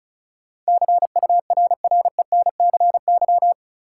Morse code, heard as a keyed tone, about 34 wpm, 700 Hz. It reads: CURRENCY
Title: Sentences from Top 1500 Words - New Word Drill Alt - 35wpm